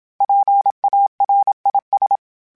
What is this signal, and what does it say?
Morse code, heard as a keyed tone, about 27 wpm, 800 Hz. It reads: PARIS
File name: PARIS Calibration / PARIS mp3s / PARIS@30